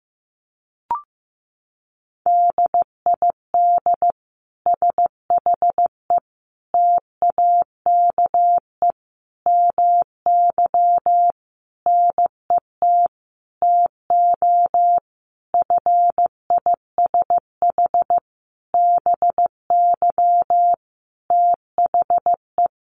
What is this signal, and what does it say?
Morse code, heard as a keyed tone, about 15 wpm, 700 Hz. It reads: DID SHE TAKE MY NET TO FISH BY THE
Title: Sentences from Top 1400 Words 15wpm